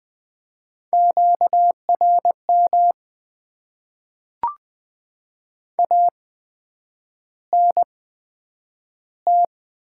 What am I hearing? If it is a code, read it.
Morse code, heard as a keyed tone, about 20 wpm, spaced as for 9 wpm, 700 Hz. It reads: QRM ANT
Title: Top 90 QSO Elements - 8x Character Spacing - 20wpm